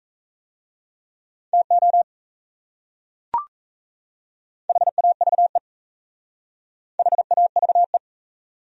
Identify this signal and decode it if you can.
Morse code, heard as a keyed tone, about 38 wpm, 700 Hz. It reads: TO HAVE HAVE